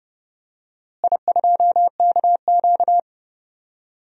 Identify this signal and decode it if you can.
Morse code, heard as a keyed tone, about 30 wpm, 700 Hz. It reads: I2KQ
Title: DXCC-100 Call Signs - 4 Characters - Repeated 3x - 30wpm